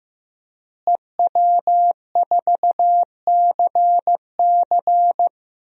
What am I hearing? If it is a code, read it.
Morse code, heard as a keyed tone, about 15 wpm, 700 Hz. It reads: EW4CC
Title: DXCC-100 Call Signs - 5 Characters - Repeated 3x - 15wpm